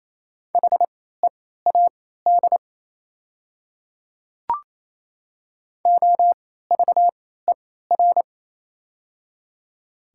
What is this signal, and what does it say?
Morse code, heard as a keyed tone, about 27 wpm, spaced as for 16 wpm, 690 Hz. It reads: HEAD OVER
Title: 3x Character Spacing - Top 500 Words - 28wpm